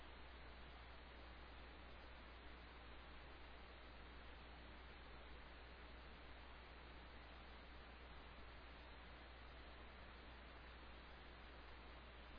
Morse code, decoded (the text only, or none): none